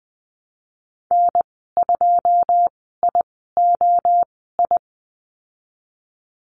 Morse code, heard as a keyed tone, about 20 wpm, spaced as for 15 wpm, 700 Hz. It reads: N2IOI
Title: 207 Lesson 33g - Letter j - Call Signs 20wpm